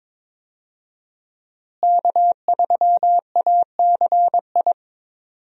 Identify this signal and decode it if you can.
Morse code, heard as a keyed tone, about 22 wpm, 700 Hz. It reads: K3ACI
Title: USA Call Signs - 5 Characters - Repeated 2x - 22wpm